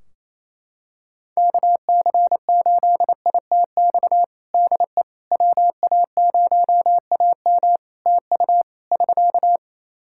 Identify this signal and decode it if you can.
Morse code, heard as a keyed tone, about 28 wpm, 700 Hz. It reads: KC8ITX DE WA0AM TU <SK>
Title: Practice QSO - Exchange 4 - Ending - 28wpm